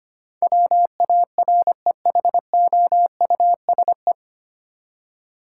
Morse code, heard as a keyed tone, about 25 wpm, 700 Hz. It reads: WAREHOUSE
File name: Nine Letter Words - 25wpm